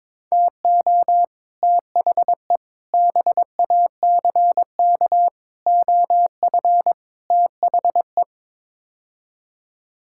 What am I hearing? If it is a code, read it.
Morse code, heard as a keyed tone, about 22 wpm, 700 Hz. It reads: TO THE BACK OF THE
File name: practice / 22wpm / Build A Phrase - 5-Word Phrases - 22wpm